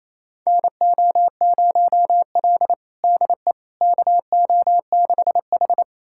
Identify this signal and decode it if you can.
Morse code, heard as a keyed tone, about 28 wpm, 700 Hz. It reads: NO0L DE KO6H
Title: USA Callsign DE Callsign - 4 Characters - 28wpm